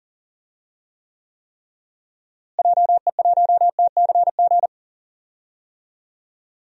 Morse code, heard as a keyed tone, about 40 wpm, 700 Hz. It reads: JE1TCG
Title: DXCC-100 Call Signs - 6 Characters - Repeated 3x - 40wpm